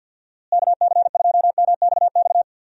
Morse code, heard as a keyed tone, about 47 wpm, 700 Hz. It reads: KC2MXX
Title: USA Call Signs - 6 Characters - Repeated 3x - 50wpm